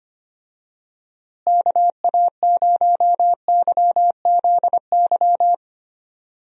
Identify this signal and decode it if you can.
Morse code, heard as a keyed tone, about 25 wpm, 700 Hz. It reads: KA0YZY
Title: USA Call Signs - 6 Characters - Repeated 3x - 25wpm